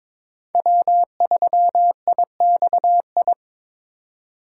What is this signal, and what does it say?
Morse code, heard as a keyed tone, about 21 wpm, 700 Hz. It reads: W3IXI
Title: USA Call Signs - 4x Character Spacing - 22wpm